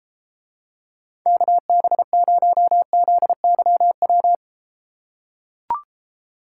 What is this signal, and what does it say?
Morse code, heard as a keyed tone, about 34 wpm, 700 Hz. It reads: KB0ZYW